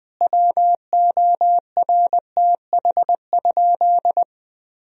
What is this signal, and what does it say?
Morse code, heard as a keyed tone, about 21 wpm, 700 Hz. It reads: WORTH?
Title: Sentences from Top 1000 Words - Review Part 1 - 20wpm